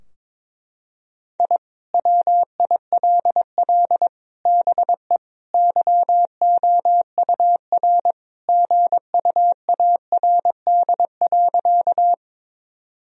Morse code, heard as a keyed tone, about 21 wpm, 700 Hz. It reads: I WILL BE YOUR GUARD.
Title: Sentences from Top 1000 Words - Review Part 4 - 22wpm